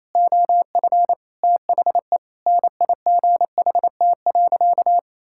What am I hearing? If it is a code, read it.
Morse code, heard as a keyed tone, about 28 wpm, 700 Hz. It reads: OF THE NIGHT.
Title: Sentences from Top 1000 Words - Review Part 6 - 28wpm